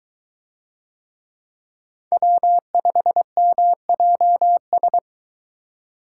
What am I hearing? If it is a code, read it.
Morse code, heard as a keyed tone, about 22 wpm, 700 Hz. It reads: W5MJS